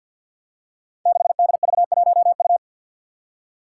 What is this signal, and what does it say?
Morse code, heard as a keyed tone, about 52 wpm, 700 Hz. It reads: 6DV1U
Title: Five Challenging Letter-Number Combinations 50wpm